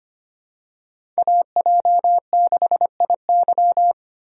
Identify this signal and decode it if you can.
Morse code, heard as a keyed tone, about 25 wpm, 700 Hz. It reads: AJ6IY
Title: USA Call Signs - 5 Characters - 4x Character Spacing - 25wpm